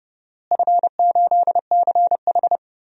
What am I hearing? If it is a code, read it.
Morse code, heard as a keyed tone, about 30 wpm, 700 Hz. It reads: F8CH
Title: DXCC-100 Call Signs - 4 Characters - Repeated 3x - 30wpm